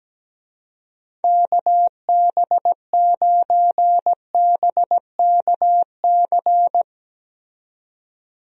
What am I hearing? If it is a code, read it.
Morse code, heard as a keyed tone, about 17 wpm, 700 Hz. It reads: KB9BKC
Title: USA Call Signs - 6 Characters - 17wpm